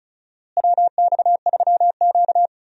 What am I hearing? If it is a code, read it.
Morse code, heard as a keyed tone, about 34 wpm, 700 Hz. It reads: WX3Q
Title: USA Call Signs - 4 Characters - Repeated 2x - 35wpm